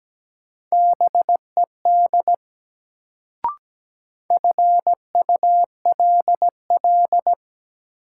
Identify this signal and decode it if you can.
Morse code, heard as a keyed tone, about 17 wpm, 700 Hz. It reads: BED FULL
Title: Sentences from Top 500 Words - New Word Drill - 17wpm